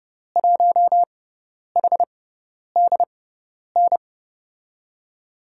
Morse code, heard as a keyed tone, about 30 wpm, spaced as for 10 wpm, 700 Hz. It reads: 1HDN
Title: USA Call Signs - 6 Characters - 6x Character Spacing - 30wpm